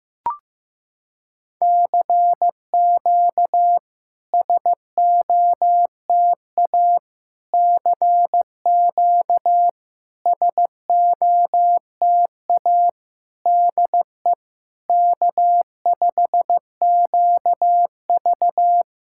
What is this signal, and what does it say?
Morse code, heard as a keyed tone, about 15 wpm, 700 Hz. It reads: CQ SOTA CQ SOTA DE K5QV
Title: SOTA QSO Practice 15wpm